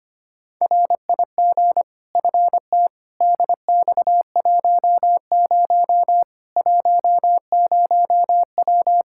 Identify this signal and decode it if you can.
Morse code, heard as a keyed tone, about 25 wpm, 700 Hz. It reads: RIG FT DX10 10W
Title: Practice QSO - Exchange 2 - Info - 25wpm